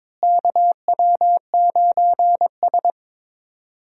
Morse code, heard as a keyed tone, about 21 wpm, 700 Hz. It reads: KW9S
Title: USA Call Signs - 4 Characters - Repeated 2x - 22wpm